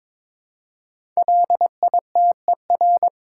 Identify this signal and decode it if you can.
Morse code, heard as a keyed tone, about 22 wpm, 700 Hz. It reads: LITER
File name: 5 Letter Words - First 3 Letters Given - 22wpm